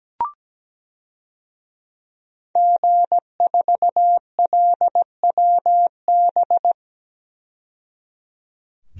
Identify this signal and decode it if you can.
Morse code, heard as a keyed tone, about 17 wpm, 700 Hz. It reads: G4LWB
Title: DXCC-100 Call Signs - 5 Characters - 17wpm